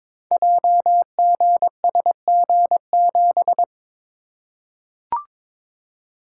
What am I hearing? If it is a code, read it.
Morse code, heard as a keyed tone, about 22 wpm, 700 Hz. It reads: JGSG7
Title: Code Groups - 5 Letter:Numbers - 22wpm